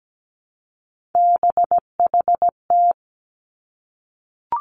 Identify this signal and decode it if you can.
Morse code, heard as a keyed tone, about 17 wpm, 700 Hz. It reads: BHT
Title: Three Challenging Letter-Number Combinations 17wpm